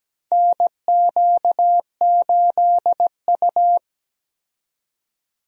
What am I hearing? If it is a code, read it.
Morse code, heard as a keyed tone, about 17 wpm, 700 Hz. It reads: NQ8U
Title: USA Call Signs - 4 Characters - 17wpm